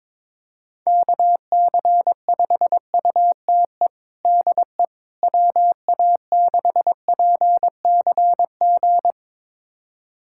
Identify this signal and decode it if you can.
Morse code, heard as a keyed tone, about 22 wpm, 700 Hz. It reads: KC5UTE DE WA6PCG
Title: USA Callsign DE Callsign 22wpm